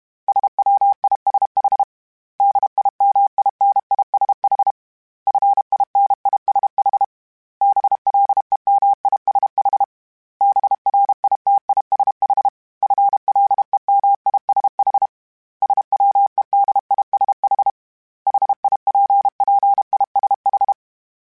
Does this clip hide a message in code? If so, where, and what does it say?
Yes: Morse code from 0.3 to 20.7 s, heard as a keyed tone, about 32 wpm, 800 Hz. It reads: SWISH DIMINISH FINISH BLEMISH BRITISH FLEMISH SWEDISH HIPPISH